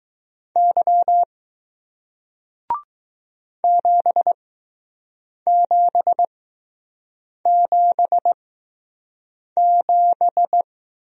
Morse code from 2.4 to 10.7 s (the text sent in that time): 7777